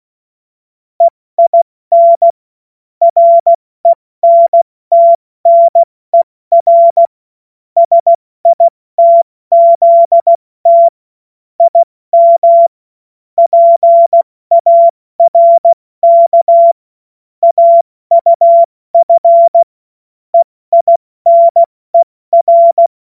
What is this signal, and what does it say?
Morse code, heard as a keyed tone, about 16 wpm, 680 Hz. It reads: EIN RENTNER SITZT IM PARK AUF EINER
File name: Morsekurs / Witze / Witz 22. August 2023 70 F80